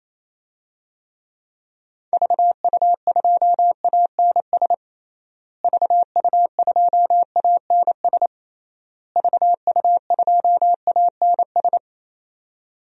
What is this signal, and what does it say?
Morse code, heard as a keyed tone, about 28 wpm, 700 Hz. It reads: VU2ANS VU2ANS VU2ANS
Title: DXCC-100 Call Signs - 6 Characters - Repeated 3x - 28wpm